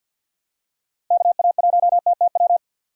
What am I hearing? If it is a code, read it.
Morse code, heard as a keyed tone, about 46 wpm, 700 Hz. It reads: KA1TTW
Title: USA Call Signs - 6 Characters - 3x Character Spacing - 50wpm